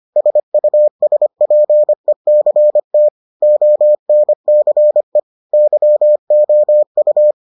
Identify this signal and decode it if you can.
Morse code, heard as a keyed tone, about 25 wpm, 600 Hz. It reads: SUSPECT ONCE YOU